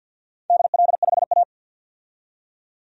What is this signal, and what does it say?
Morse code, heard as a keyed tone, about 53 wpm, 700 Hz. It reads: DL5A
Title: DXCC-100 Call Signs - 4 Characters - 50wpm